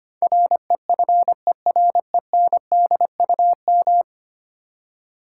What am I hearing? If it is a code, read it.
Morse code, heard as a keyed tone, about 25 wpm, 700 Hz. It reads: REFERENDUM
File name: 10 Letter Words - First 3 Letters Given - 25wpm